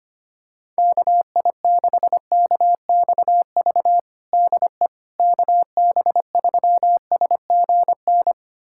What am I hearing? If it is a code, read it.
Morse code, heard as a keyed tone, about 25 wpm, 700 Hz. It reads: KI6KXV DE KB3SGN
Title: USA Callsign DE Callsign 25wpm